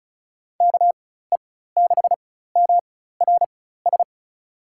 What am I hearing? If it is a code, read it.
Morse code, heard as a keyed tone, about 34 wpm, spaced as for 16 wpm, 690 Hz. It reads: KE6MRS
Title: USA Call Signs - 6 Characters - 4x Character Spacing - 35wpm